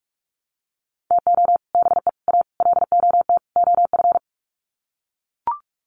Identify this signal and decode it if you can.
Morse code, heard as a keyed tone, about 44 wpm, 700 Hz. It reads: TO BE A LOT OF
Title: Sets of 5 Words Encore 45wpm